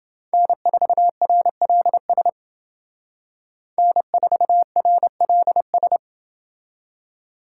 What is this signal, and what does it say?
Morse code, heard as a keyed tone, about 28 wpm, 700 Hz. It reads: N4RLS N4RLS